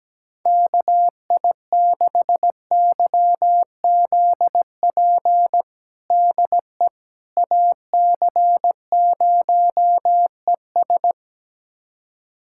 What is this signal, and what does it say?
Morse code, heard as a keyed tone, about 17 wpm, 700 Hz. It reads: KI6YZP DE AC0ES
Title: USA Callsign DE Callsign 17wpm